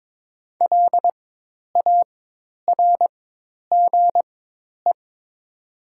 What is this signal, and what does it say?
Morse code, heard as a keyed tone, about 22 wpm, spaced as for 10 wpm, 700 Hz. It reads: LARGE
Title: 4x Character Spacing - Top 200 Words - 22wpm